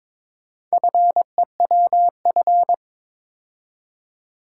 Morse code, heard as a keyed tone, about 21 wpm, 700 Hz. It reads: FEWF
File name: practice / 22wpm / Four Letter-Number Combinations - 22wpm